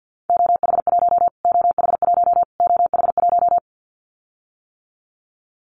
Morse code, heard as a keyed tone, about 50 wpm, 700 Hz. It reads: OH1 OH1 OH1
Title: Callsign Prefixes for Top 25 DXCC - Repeated 3x - 50wpm